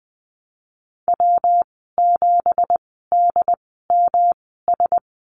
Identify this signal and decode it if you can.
Morse code, heard as a keyed tone, about 19 wpm, spaced as for 14 wpm, 700 Hz. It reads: W7DMS
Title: 173 Lesson 28g - Number 9 - Call Signs 20wpm